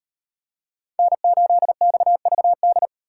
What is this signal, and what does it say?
Morse code, heard as a keyed tone, about 30 wpm, 700 Hz. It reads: N8XVD
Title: Speed Racing - USA 1x3 Call Signs 25wpm